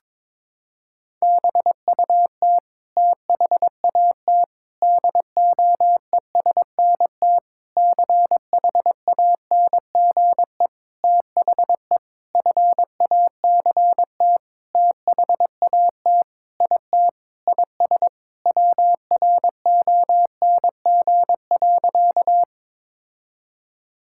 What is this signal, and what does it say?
Morse code, heard as a keyed tone, about 22 wpm, 700 Hz. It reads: BUT THAT DOESNT CHANGE THE FACT THAT IT IS WRONG.
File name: Sentences with 7-Word Phrases - 22wpm